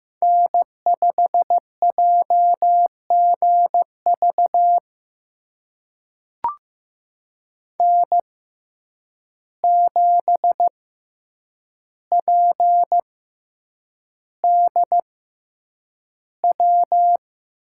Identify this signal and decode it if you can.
Morse code, heard as a keyed tone, about 15 wpm, spaced as for 8 wpm, 700 Hz. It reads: N5JGV N7PDW
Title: USA Call Signs - 5 Characters - 6x Character Spacing - 15wpm